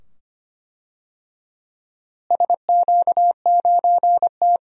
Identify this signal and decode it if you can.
Morse code, heard as a keyed tone, about 25 wpm, 700 Hz. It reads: SQ9T
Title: DXCC-100 Call Signs - 4 Characters - Repeated 2x - 25wpm